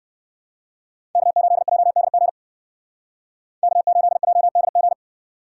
Morse code, heard as a keyed tone, about 66 wpm, 700 Hz. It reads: K8JDG K8JDG